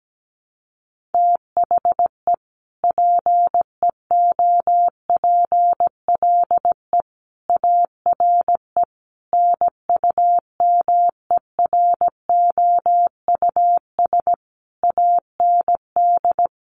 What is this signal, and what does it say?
Morse code, heard as a keyed tone, about 17 wpm, 700 Hz. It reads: THE PEOPLE ARE NUMEROUS AND
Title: Sentences from Top 1400 Words 17wpm